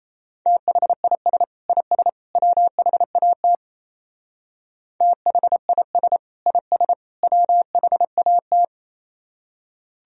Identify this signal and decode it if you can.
Morse code, heard as a keyed tone, about 34 wpm, 700 Hz. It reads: THIS IS WHAT THIS IS WHAT
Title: Speed-Racing - Sets of 3 Words 28wpm